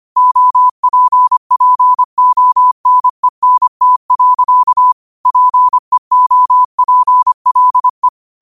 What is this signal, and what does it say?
Morse code, heard as a keyed tone, about 25 wpm, 1000 Hz. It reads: OPPONENT. PEOPLE
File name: qotdfiles / 25wpm